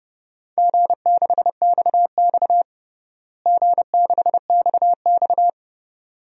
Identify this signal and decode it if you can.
Morse code, heard as a keyed tone, about 30 wpm, 700 Hz. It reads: G6XX G6XX
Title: DXCC-100 Call Signs - 4 Characters - Repeated 2x - 30wpm